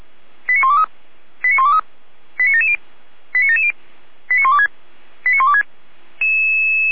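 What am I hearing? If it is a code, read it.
Five-tone selective call (ZVEI), heard as quick runs of five tones, tones of about 70 ms. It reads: 89123 89123 89770 89770 89126 89126
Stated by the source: ZVEI five-tone selective call 89706, received by SDR on the 455 channel.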